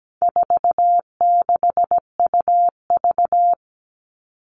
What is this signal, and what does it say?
Morse code, heard as a keyed tone, about 17 wpm, 700 Hz. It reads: 46UV
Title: All Cut Numbers - 4-Digit 17wpm